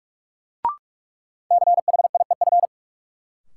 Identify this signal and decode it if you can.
Morse code, heard as a keyed tone, about 47 wpm, 700 Hz. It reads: CHIEF
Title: Sets of 5 Words Encore - New Word Drill - 45wpm